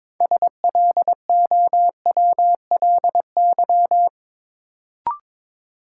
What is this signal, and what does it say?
Morse code, heard as a keyed tone, about 22 wpm, 700 Hz. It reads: SLOWLY
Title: Sentences from Top 1100 Words - New Word Drill - 22wpm